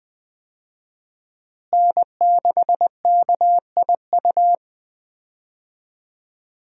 Morse code, heard as a keyed tone, about 20 wpm, 700 Hz. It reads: N6KIU
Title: USA Call Signs - 5 Characters - Repeated 3x - 20wpm